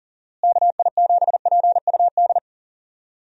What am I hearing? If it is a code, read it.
Morse code, heard as a keyed tone, about 40 wpm, 700 Hz. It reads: KI7PUD
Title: USA Call Signs - 6 Characters - Repeated 2x - 40wpm